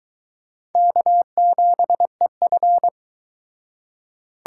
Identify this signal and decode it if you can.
Morse code, heard as a keyed tone, about 22 wpm, 700 Hz. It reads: K7EF